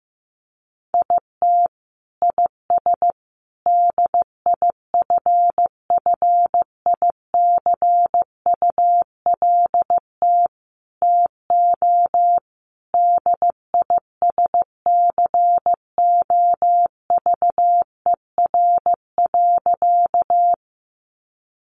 Morse code, heard as a keyed tone, about 15 wpm, 700 Hz. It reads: IT IS DIFFICULT TO DISCOVER.
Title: Sentences from Top 1100 Words 15wpm